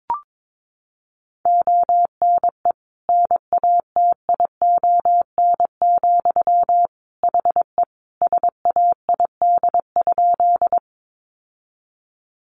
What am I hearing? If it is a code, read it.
Morse code, heard as a keyed tone, about 22 wpm, 700 Hz. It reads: ONE NATION, HE SAID?
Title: Sentences from Top 900 Words 22wpm